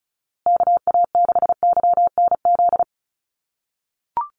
Morse code, heard as a keyed tone, about 34 wpm, 700 Hz. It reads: KA6YNZ
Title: USA Call Signs Repeated 2x 35wpm